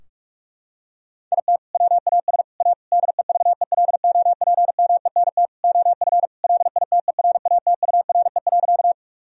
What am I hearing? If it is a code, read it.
Morse code, heard as a keyed tone, about 46 wpm, 700 Hz. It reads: IT WAS A DEVELOPMENT OF LITERATURE.
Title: Sentences from Top 1000 Words - Review Part 4 - 45wpm